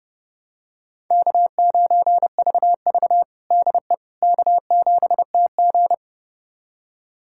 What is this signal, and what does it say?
Morse code, heard as a keyed tone, about 32 wpm, 700 Hz. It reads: K9VV DE K7TG
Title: USA Callsign DE Callsign - 4 Characters - 30wpm